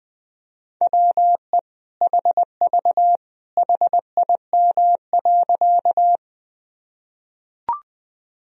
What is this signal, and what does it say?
Morse code, heard as a keyed tone, about 20 wpm, 700 Hz. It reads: WE HV HIM.
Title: Sentences from Top 100 Words and Abbreviations 20wpm